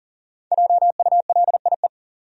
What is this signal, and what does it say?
Morse code, heard as a keyed tone, about 40 wpm, 700 Hz. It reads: JULIE
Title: USA Common Names - Max 5 Letters - 40wpm